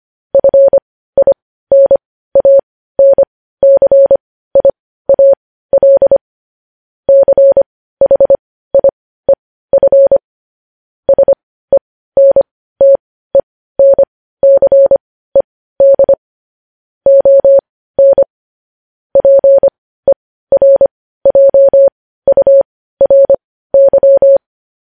Morse code, heard as a keyed tone, about 25 wpm, spaced as for 15 wpm, 550 Hz. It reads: FINANCIAL CHIEF SENTENCED ON PERJURY